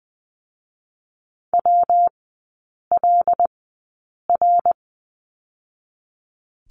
Morse code, heard as a keyed tone, about 20 wpm, spaced as for 8 wpm, 700 Hz. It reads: WLR
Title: 166 Lesson 27e - Number 7 - Three Characters 20wpm